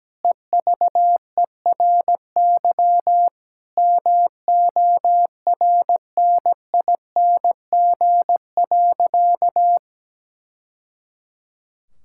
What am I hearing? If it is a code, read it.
Morse code, heard as a keyed tone, about 17 wpm, 700 Hz. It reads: EVERY MORNING.
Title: Sentences from Top 1000 Words - Review Part 6 - 17wpm